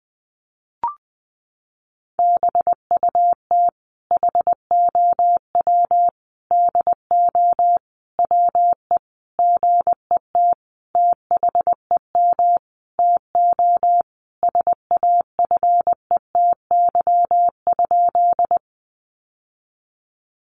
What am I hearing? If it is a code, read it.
Morse code, heard as a keyed tone, about 20 wpm, 700 Hz. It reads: BUT HOW DO WE GET THEM TO SAFETY?
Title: Sentences from Top 700 Words 20wpm